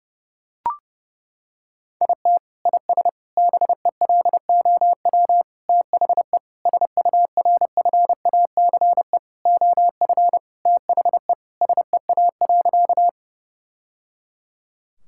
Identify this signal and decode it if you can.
Morse code, heard as a keyed tone, about 30 wpm, 700 Hz. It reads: IT IS BELOW THE SURFACE OF THE SEA.
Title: Sentences from Top 1000 Words - Review Part 5 - 30wpm